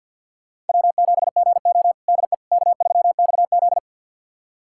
Morse code, heard as a keyed tone, about 52 wpm, 700 Hz. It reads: W7GO DE K3XZ
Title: USA Callsign DE Callsign - 4 Characters - 50wpm